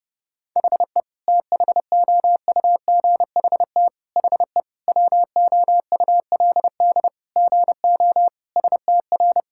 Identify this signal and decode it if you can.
Morse code, heard as a keyed tone, about 30 wpm, 700 Hz. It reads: HE THOUGHT HE WOULD GO STR
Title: Sentences from Top 1000 Words - Review Part 6 - 30wpm